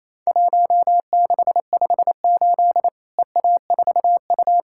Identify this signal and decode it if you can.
Morse code, heard as a keyed tone, about 28 wpm, 700 Hz. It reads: 1658 EA4U
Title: Worked All Europe DX Contest - QTC Exchange - 28wpm